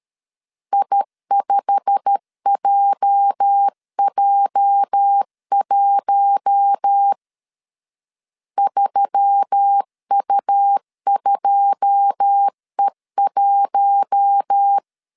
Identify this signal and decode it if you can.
Morse code, heard as a keyed tone, about 13 wpm, 800 Hz. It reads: I5JJ1 3U2E1
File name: CW / DET / 1 9 1500C S C 12wpm 3811